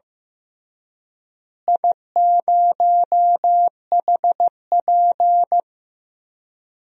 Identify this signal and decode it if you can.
Morse code, heard as a keyed tone, about 15 wpm, 700 Hz. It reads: I0HP
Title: DXCC-100 Call Signs - 4 Characters - Repeated 3x - 15wpm